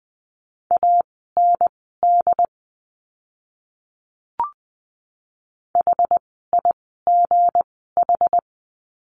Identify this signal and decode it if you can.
Morse code, heard as a keyed tone, about 20 wpm, spaced as for 14 wpm, 700 Hz. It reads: AND HIGH
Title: 160 Lesson 26f - Letter g - Words 20wpm